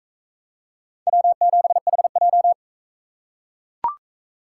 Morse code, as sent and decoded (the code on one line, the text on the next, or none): .-- --... .... .---
W7HJ